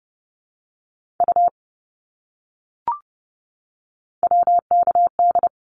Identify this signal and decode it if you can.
Morse code, heard as a keyed tone, about 30 wpm, 700 Hz. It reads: U WKD
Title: Abbreviations Repeated 3x - 30wpm